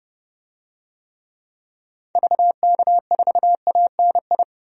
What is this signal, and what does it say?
Morse code, heard as a keyed tone, about 30 wpm, 700 Hz. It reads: VK4ANI
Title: DXCC-100 Call Signs - 6 Characters - Repeated 2x - 30wpm